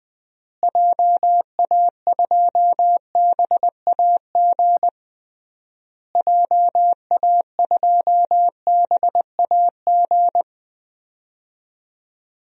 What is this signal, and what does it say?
Morse code, heard as a keyed tone, about 19 wpm, 700 Hz. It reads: JA2BAG JA2BAG
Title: DXCC-100 Call Signs - 6 Characters - Repeated 3x - 20wpm